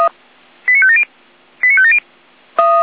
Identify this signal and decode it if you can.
Five-tone selective call (ZVEI), heard as quick runs of five tones, tones of about 70 ms. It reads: 89570 89570
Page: /455/SDR/ZVEI/